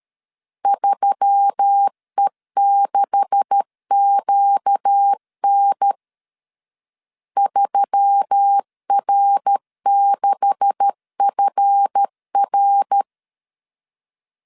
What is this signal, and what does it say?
Morse code, heard as a keyed tone, about 13 wpm, 800 Hz. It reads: 3E6QN 3R6FR
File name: CW / DET / 1 9 1500C S C 12wpm 3811